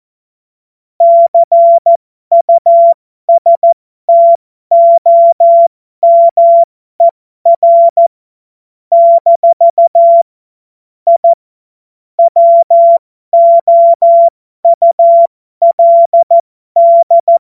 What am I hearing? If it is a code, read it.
Morse code, heard as a keyed tone, about 14 wpm, spaced as for 12 wpm, 680 Hz. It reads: CUSTOMER - I WOULD
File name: Morsekurs / Witze